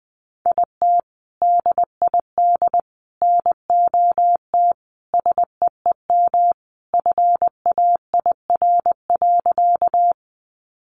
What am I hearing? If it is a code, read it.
Morse code, heard as a keyed tone, about 20 wpm, 700 Hz. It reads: IT DID NOT SEEM FAIR.
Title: Sentences from Top 1300 Words 20wpm